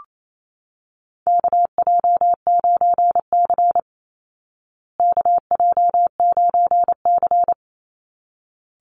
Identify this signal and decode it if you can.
Morse code, heard as a keyed tone, about 28 wpm, 700 Hz. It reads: KJ9C KJ9C
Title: Call Signs Repeated 3x - 28wpm